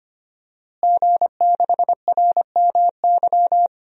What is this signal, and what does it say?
Morse code, heard as a keyed tone, about 25 wpm, 700 Hz. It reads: G6RMY
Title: DXCC-100 Call Signs - 5 Characters - Repeated 2x - 25wpm